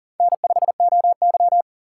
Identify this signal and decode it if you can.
Morse code, heard as a keyed tone, about 40 wpm, 700 Hz. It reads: N5OY
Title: USA Call Signs - 3x Character Spacing - 40wpm